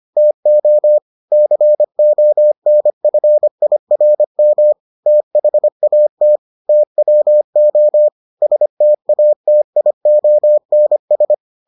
Morse code, heard as a keyed tone, about 25 wpm, 600 Hz. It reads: TO CONFIRM THAT TWO STATIONS